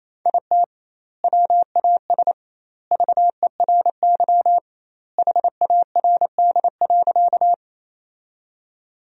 Morse code, heard as a keyed tone, about 28 wpm, 700 Hz. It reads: IT WAS VERY HARD.
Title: 2x Word-Spacing - Sentences from Top 300 Words - 28wpm